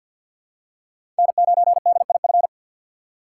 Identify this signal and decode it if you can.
Morse code, heard as a keyed tone, about 48 wpm, 700 Hz. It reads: N9DIF
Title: USA Call Signs - 5 Characters - 6x Character Spacing - 50wpm